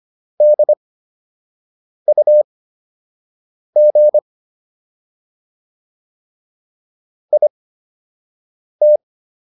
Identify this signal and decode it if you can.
Morse code, heard as a keyed tone, about 25 wpm, spaced as for 6 wpm, 600 Hz. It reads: DUG IT